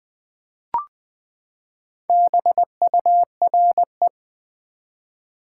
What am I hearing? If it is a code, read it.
Morse code, heard as a keyed tone, about 20 wpm, 700 Hz. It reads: BURE
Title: Build A Word - 4 Letters At A Time - 11-Letter Words - 20wpm